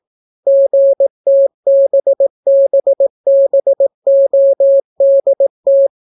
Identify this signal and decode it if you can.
Morse code, heard as a keyed tone, about 18 wpm, 550 Hz. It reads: GTBBBODT